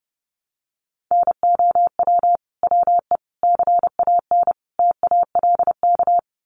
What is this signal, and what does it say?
Morse code, heard as a keyed tone, about 28 wpm, 700 Hz. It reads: NOW WE CAN TALK
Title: Sentences from Top 700 Words 30wpm